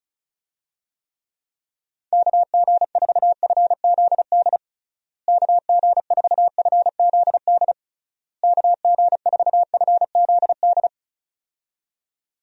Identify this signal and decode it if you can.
Morse code, heard as a keyed tone, about 36 wpm, 700 Hz. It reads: KG4FZD KG4FZD KG4FZD
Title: USA Call Signs - 6 Characters - Repeated 3x - 35wpm